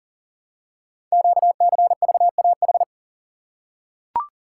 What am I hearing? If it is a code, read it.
Morse code, heard as a keyed tone, about 38 wpm, 700 Hz. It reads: QCVAH
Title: Code Groups - 5 Letters - 40wpm